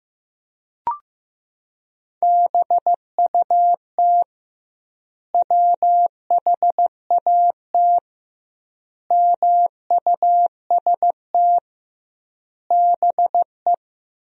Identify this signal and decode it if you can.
Morse code, heard as a keyed tone, about 15 wpm, 700 Hz. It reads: BUT WHAT MUST BE
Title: 2x Word-Spacing - Sentences from Top 500 Words - 15wpm